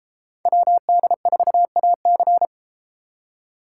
Morse code, heard as a keyed tone, about 31 wpm, 700 Hz. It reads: WD4AC